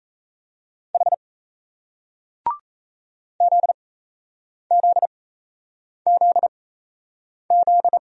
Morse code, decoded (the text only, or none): H ZZZZ